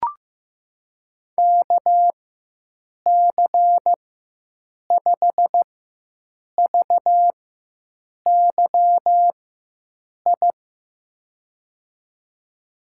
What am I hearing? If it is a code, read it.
Morse code, heard as a keyed tone, about 15 wpm, spaced as for 7 wpm, 700 Hz. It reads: KC5VYI
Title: USA Call Signs - 6 Characters - 4x Character Spacing - 15wpm